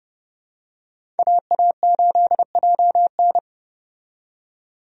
Morse code, heard as a keyed tone, about 30 wpm, 700 Hz. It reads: AA8JN
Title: USA Call Signs - 5 Characters - 30wpm